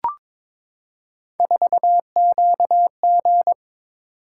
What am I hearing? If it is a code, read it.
Morse code, heard as a keyed tone, about 22 wpm, 700 Hz. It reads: 4QG
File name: practice / 22wpm / Three Letter-Number Combinations - 22wpm